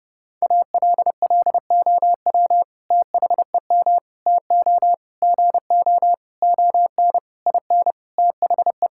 Morse code, heard as a keyed tone, about 29 wpm, 700 Hz. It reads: ALLOW THEM TO GO ON IN THE
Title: Sentences from Top 1000 Words - Review Part 5 - 30wpm